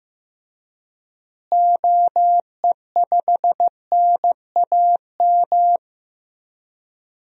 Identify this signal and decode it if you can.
Morse code, heard as a keyed tone, about 15 wpm, 700 Hz. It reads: OE5NAM
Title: DXCC-100 Call Signs - 6 Characters - Repeated 2x - 15wpm